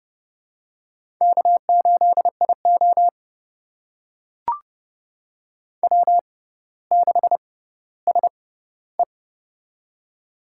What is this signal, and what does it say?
Morse code, heard as a keyed tone, about 30 wpm, spaced as for 15 wpm, 690 Hz. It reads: K8IO W6SE
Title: USA Call Signs - 4 Characters - 6x Character Spacing - 30wpm